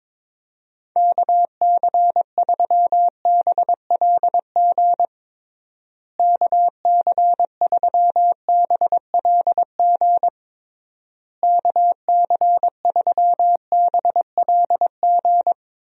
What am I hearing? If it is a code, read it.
Morse code, heard as a keyed tone, about 21 wpm, 700 Hz. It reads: KC3BLG KC3BLG KC3BLG
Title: USA Call Signs - 6 Characters - Repeated 3x - 22wpm